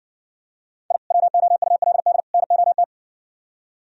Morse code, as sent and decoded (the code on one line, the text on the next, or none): .. / .-- --- ..- .-.. -.. / -. --- -
I WOULD NOT